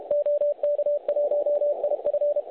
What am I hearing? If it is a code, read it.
Morse code, heard as a keyed tone, about 31 wpm, 590 Hz. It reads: OK1IF
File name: truSDX / websdr recording 3541.9kHz